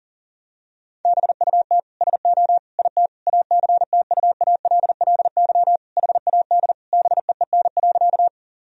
Morse code, heard as a keyed tone, about 40 wpm, 700 Hz. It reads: BUT SO IT ACTUALLY HAD BEEN.
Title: Sentences from Top 1000 Words - Review Part 4 - 40wpm